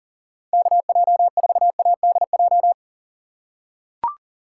Morse code, heard as a keyed tone, about 40 wpm, 690 Hz. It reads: KJ4ADJ
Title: USA Call Signs - 6 Characters - 2x Character Spacing - 40wpm